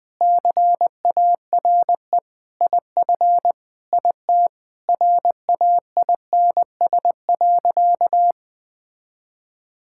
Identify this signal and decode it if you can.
Morse code, heard as a keyed tone, about 20 wpm, 700 Hz. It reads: CARE IF IT RAINS.
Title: Sentences with 5-Word Phrases - Spoken Beginning - 20wpm